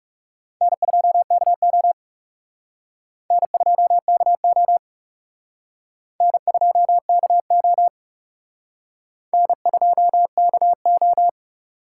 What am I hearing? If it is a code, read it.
Morse code, heard as a keyed tone, about 38 wpm, 700 Hz. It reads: N2KO N2KO N2KO N2KO